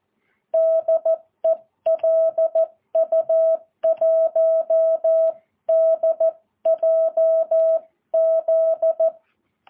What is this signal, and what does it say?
Morse code, heard as a keyed tone, about 14 wpm, 650 Hz. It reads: DE LU1DJZ
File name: jh8cbh